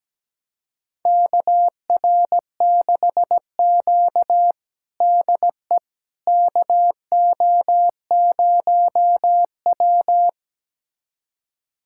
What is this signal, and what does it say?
Morse code, heard as a keyed tone, about 17 wpm, 700 Hz. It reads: KR6Q DE KO0W
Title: USA Callsign DE Callsign - 4 Characters - 17wpm